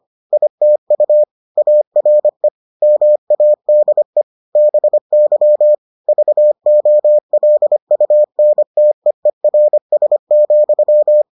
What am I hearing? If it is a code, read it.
Morse code, heard as a keyed tone, about 25 wpm, 600 Hz. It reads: ITU ARE MADE BY VOLUNTEERS,